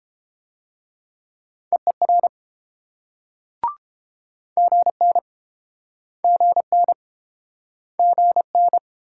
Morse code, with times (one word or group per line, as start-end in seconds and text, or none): 1.7-2.3 EER
4.6-5.2 GN
6.2-6.9 GN
8.0-8.8 GN